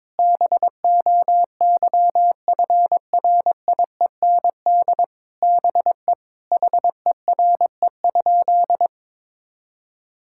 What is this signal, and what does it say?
Morse code, heard as a keyed tone, about 21 wpm, 700 Hz. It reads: BOYFRIEND BE HERE?
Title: Sentences from Top 1000 Words - Review Part 5 - 22wpm